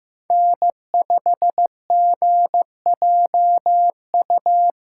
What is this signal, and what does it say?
Morse code, heard as a keyed tone, about 15 wpm, 700 Hz. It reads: N5GJU
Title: USA Callsign DE Callsign - 5 Characters - 15wpm